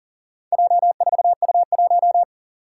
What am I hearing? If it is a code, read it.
Morse code, heard as a keyed tone, about 39 wpm, 700 Hz. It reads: J4U1
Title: Four Challenging Letter-Number Combinations 40wpm